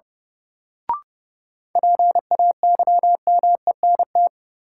Morse code, heard as a keyed tone, about 30 wpm, 700 Hz. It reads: PAYMENT
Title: Sentences from Top 1200 Words - New Word Drill - 30wpm